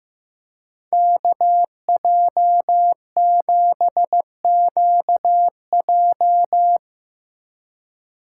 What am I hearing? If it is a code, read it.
Morse code, heard as a keyed tone, about 15 wpm, 700 Hz. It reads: KJ7QJ
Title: Speed Racing - USA Call Signs 15wpm